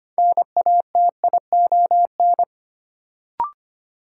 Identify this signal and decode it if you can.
Morse code, heard as a keyed tone, about 25 wpm, 700 Hz. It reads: NATION
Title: Sentences from Top 900 Words - New Word Drill - 25wpm